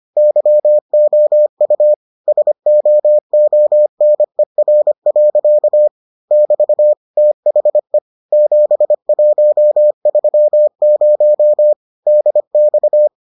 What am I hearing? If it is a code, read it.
Morse code, heard as a keyed tone, about 25 wpm, 600 Hz. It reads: YOU SOONER. <BT> THE 7130 DX